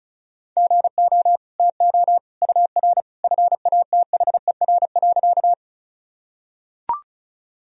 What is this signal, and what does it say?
Morse code, heard as a keyed tone, about 34 wpm, 700 Hz. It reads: GO TO UR FATHER.
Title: Sentences from Top 300 Words and Abbreviations 35wpm